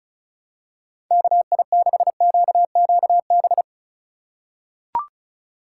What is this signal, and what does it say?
Morse code, heard as a keyed tone, about 34 wpm, 700 Hz. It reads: KI6QQB
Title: USA Call Signs - 6 Characters - 2x Character Spacing - 35wpm